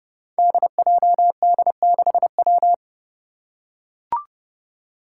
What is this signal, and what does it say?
Morse code, heard as a keyed tone, about 30 wpm, 700 Hz. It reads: DJD6W
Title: Five Challenging Letter-Number Combinations 30wpm